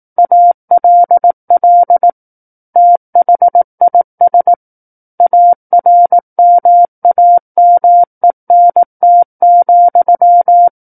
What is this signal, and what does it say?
Morse code, heard as a keyed tone, about 18 wpm, 700 Hz. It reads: ALL THIS ARMAMENT,
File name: 18 WPM